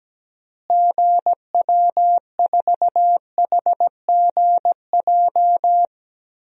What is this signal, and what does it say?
Morse code, heard as a keyed tone, about 17 wpm, 700 Hz. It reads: GW4HGJ
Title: DXCC-100 Call Signs - 6 Characters - Repeated 2x - 17wpm